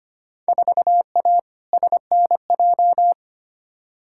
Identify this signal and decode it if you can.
Morse code, heard as a keyed tone, about 25 wpm, 700 Hz. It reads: 4A SNJ
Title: Field Day Exchanges Encore - 25wpm